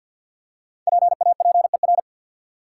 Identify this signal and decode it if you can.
Morse code, heard as a keyed tone, about 53 wpm, 700 Hz. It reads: PAPER
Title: Five Letter Words - 50wpm